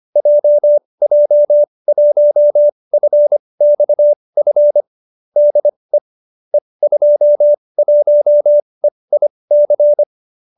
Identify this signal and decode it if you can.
Morse code, heard as a keyed tone, about 25 wpm, spaced as for 20 wpm, 590 Hz. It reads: JJ1FXF DE E21EIC
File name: qso208b 20